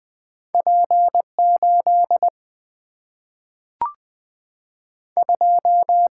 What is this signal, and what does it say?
Morse code, heard as a keyed tone, about 20 wpm, 700 Hz. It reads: P 8 2
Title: Single-Letter Single-Number 20wpm